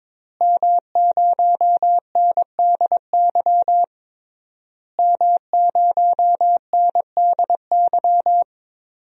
Morse code, heard as a keyed tone, about 22 wpm, 700 Hz. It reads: M0NDY M0NDY
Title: DXCC-100 Call Signs - 5 Characters - Repeated 3x - 22wpm